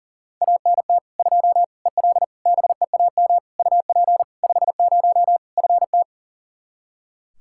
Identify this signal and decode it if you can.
Morse code, heard as a keyed tone, about 40 wpm, 700 Hz. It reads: ANT 2 EL BEAM UP 50 FT
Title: Practice QSO - Exchange 2 - Info - 40wpm